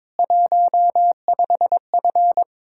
Morse code, heard as a keyed tone, about 22 wpm, 700 Hz. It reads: 15F
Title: Speed-Racing Common QSO Phrases Encore - 22wpm